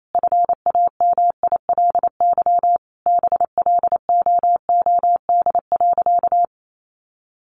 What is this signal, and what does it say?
Morse code, heard as a keyed tone, about 28 wpm, 700 Hz. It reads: FAMILY BLOOD.
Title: Sentences from Top 1200 Words 28wpm